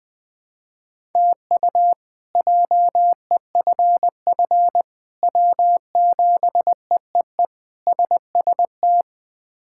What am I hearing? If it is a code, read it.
Morse code, heard as a keyed tone, about 20 wpm, 700 Hz. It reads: TU JEFF W7EEE SST
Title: K1USN SST QSO Practice - 20wpm